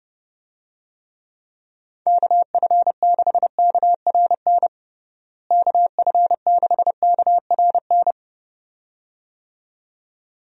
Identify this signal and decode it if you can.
Morse code, heard as a keyed tone, about 32 wpm, 700 Hz. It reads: KF6KRN KF6KRN
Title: USA Call Signs - 6 Characters - Repeated 2x - 30wpm